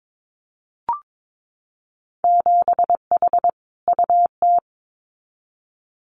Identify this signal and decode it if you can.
Morse code, heard as a keyed tone, about 22 wpm, 700 Hz. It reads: 7H UT
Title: Winter Field Day Exchanges Encore 22wpm